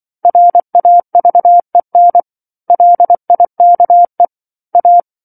Morse code, heard as a keyed tone, about 24 wpm, 700 Hz. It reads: RAVEN LIKE A
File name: 07-Chapter-07 24 WPM